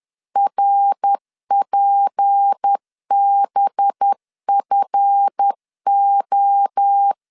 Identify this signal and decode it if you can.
Morse code, heard as a keyed tone, about 10 wpm, 800 Hz. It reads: RPBFO
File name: LEZIONI / 8 YJ45 / 8 YJ45 10wpm 2549